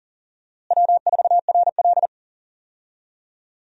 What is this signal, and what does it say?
Morse code, heard as a keyed tone, about 40 wpm, 700 Hz. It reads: W4RL
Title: USA Call Signs - 4 Characters - Repeated 2x - 40wpm